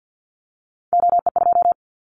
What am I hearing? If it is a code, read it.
Morse code, heard as a keyed tone, about 50 wpm, 700 Hz. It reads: OE2
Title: Callsign Prefixes for Top 50 DXCC - 50wpm